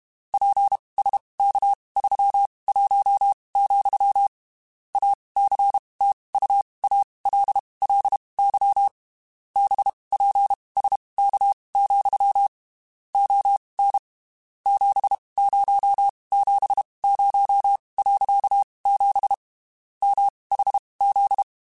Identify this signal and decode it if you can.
Morse code, heard as a keyed tone, about 32 wpm, spaced as for 23 wpm, 800 Hz. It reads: PSK31, ACTUALLY BPSK, ON 7070.7 MHZ
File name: (spaced) / QSOs Spaced-mp3s / kc7haa @30 sp